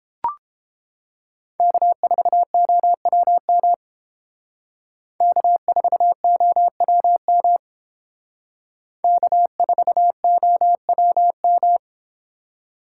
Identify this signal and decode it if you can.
Morse code, heard as a keyed tone, about 29 wpm, 700 Hz. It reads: K4OWM K4OWM K4OWM